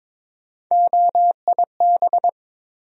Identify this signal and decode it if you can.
Morse code, heard as a keyed tone, about 22 wpm, 700 Hz. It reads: OIB
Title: USA Call Sign Postfixes - 3 Letters 22wpm